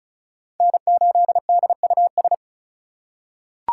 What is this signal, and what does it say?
Morse code, heard as a keyed tone, about 34 wpm, 700 Hz. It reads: N8DUS
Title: K1USN SST QSO Practice - 35wpm